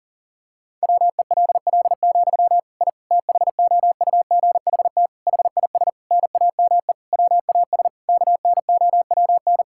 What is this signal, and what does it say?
Morse code, heard as a keyed tone, about 37 wpm, 700 Hz. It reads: WELL, I THOUGHT HIS NAME WAS KNOWN
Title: Sentences from Top 1000 Words - Review Part 5 - 40wpm